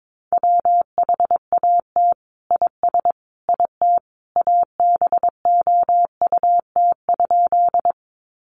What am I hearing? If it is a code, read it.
Morse code, heard as a keyed tone, about 22 wpm, 700 Hz. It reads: WHAT IS IT ABOUT?
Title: Sentences from Top 100 Words 22wpm